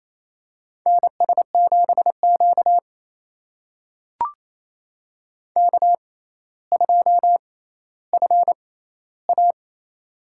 Morse code, heard as a keyed tone, about 28 wpm, spaced as for 14 wpm, 700 Hz. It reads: NS7Q K2FA
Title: USA Call Signs - 4 Characters - 6x Character Spacing - 28wpm